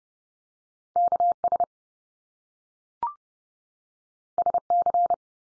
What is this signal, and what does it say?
Morse code, heard as a keyed tone, about 30 wpm, 700 Hz. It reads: KS SC
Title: US States 30WPM